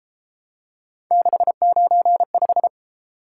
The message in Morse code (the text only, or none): 695